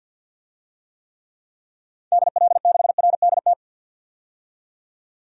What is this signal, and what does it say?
Morse code, heard as a keyed tone, about 50 wpm, 700 Hz. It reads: DL6RDT
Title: DXCC-100 Call Signs - 6 Characters - 50wpm